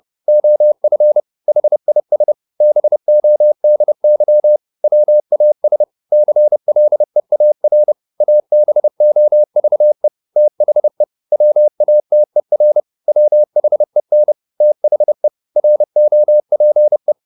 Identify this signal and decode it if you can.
Morse code, heard as a keyed tone, about 30 wpm, 600 Hz. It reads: OF HIS BODY WAS CLEAR ABOVE THE WATER WHEN THE ROPE